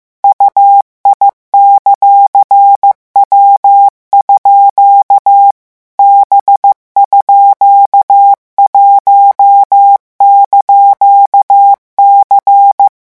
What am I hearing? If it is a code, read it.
Morse code, heard as a keyed tone, about 15 wpm, 800 Hz. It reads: UI;W_ B_1)C
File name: morse mp3 / revision17-vitesse 14 mots